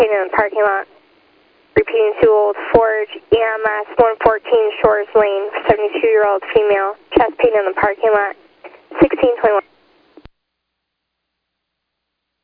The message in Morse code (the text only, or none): none